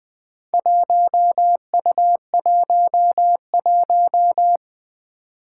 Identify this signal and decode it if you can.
Morse code, heard as a keyed tone, about 21 wpm, 700 Hz. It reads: 1U11
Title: Four Challenging Letter-Number Combinations 20wpm